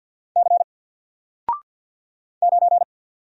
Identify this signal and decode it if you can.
Morse code, heard as a keyed tone, about 50 wpm, 700 Hz. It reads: C 9
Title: Single Character (Letter:Number:Punctuation:Prosign) 50wpm